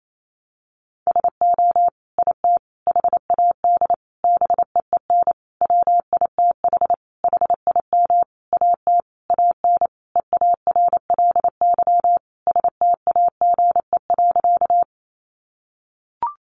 Morse code, heard as a keyed tone, about 28 wpm, 700 Hz. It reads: SO IT HAD BEEN WITH HIM AT AN EARLY STAGE.
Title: Sentences from Top 1200 Words 28wpm